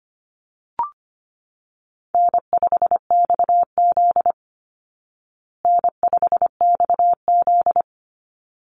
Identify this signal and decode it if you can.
Morse code, heard as a keyed tone, about 25 wpm, 700 Hz. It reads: N5XZ N5XZ
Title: Call Signs Repeated 3x - 25wpm